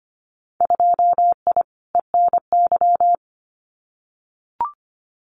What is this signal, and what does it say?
Morse code, heard as a keyed tone, about 25 wpm, 690 Hz. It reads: 2I ENY
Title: Winter Field Day Exchanges Encore 25wpm